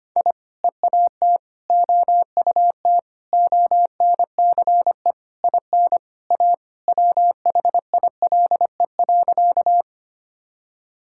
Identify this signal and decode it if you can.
Morse code, heard as a keyed tone, about 25 wpm, 700 Hz. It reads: I EAT OUT ONCE IN A WHILE.
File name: practice / 25wpm / Sentences with 4-Word Phrases - 25wpm